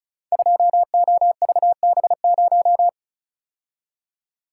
Morse code, heard as a keyed tone, about 36 wpm, 700 Hz. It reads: 2OVB0
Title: Code Groups - 5 Letter:Numbers - 35wpm